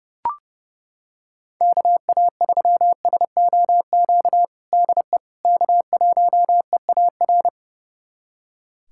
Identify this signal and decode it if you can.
Morse code, heard as a keyed tone, about 30 wpm, 700 Hz. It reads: KA3SOQ DE K1EAR
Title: USA Callsign DE Callsign 30wpm